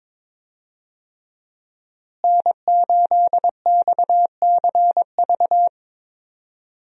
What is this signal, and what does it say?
Morse code, heard as a keyed tone, about 22 wpm, 700 Hz. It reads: N8XCV
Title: USA Call Signs - 5 Characters - Repeated 3x - 22wpm